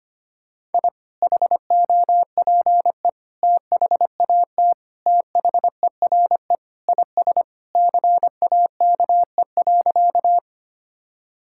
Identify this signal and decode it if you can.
Morse code, heard as a keyed tone, about 25 wpm, 700 Hz. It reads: I HOPE THAT THERE IS CAKE.
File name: practice / 25wpm / Sentences with 2-Word Phrases - 25wpm